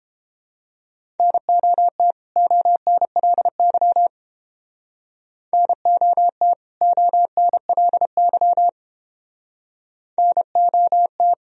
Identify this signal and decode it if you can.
Morse code, heard as a keyed tone, about 31 wpm, 700 Hz. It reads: NOT ONLY NOT ONLY NOT